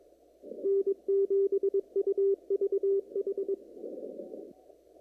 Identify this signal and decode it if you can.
Morse code, heard as a keyed tone, about 21 wpm, 410 Hz. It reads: N7UVH
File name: CW / kt5j wpx cw-op-k5pi